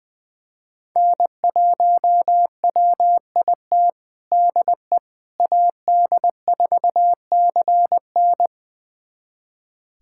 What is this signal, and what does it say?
Morse code, heard as a keyed tone, about 21 wpm, 700 Hz. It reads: N1WIT DE AD4CN
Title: USA Callsign DE Callsign - 5 Characters - 20wpm